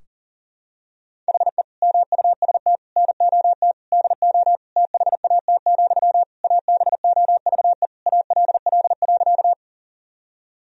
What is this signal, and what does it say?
Morse code, heard as a keyed tone, about 40 wpm, 700 Hz. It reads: HE MUST NOT DO THAT, ABOVE ALL.
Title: Sentences from Top 1000 Words - Review Part 2 - 40wpm